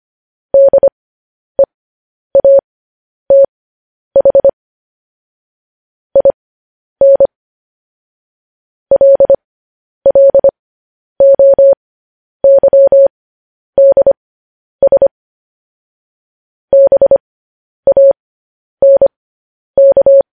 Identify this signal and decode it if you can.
Morse code, heard as a keyed tone, about 25 wpm, spaced as for 10 wpm, 550 Hz. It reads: DEATH IN LLOYDS BANK
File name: podcast / News headlines-1x-25x10